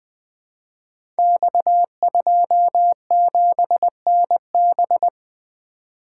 Morse code, heard as a keyed tone, about 20 wpm, 700 Hz. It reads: X27NB
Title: Code Groups - 5 Letter:Numbers - 20wpm